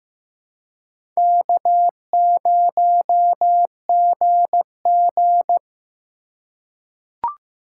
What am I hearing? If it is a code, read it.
Morse code, heard as a keyed tone, about 15 wpm, 700 Hz. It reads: K0GG
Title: USA Call Signs - 4 Characters - 2x Character Spacing - 15wpm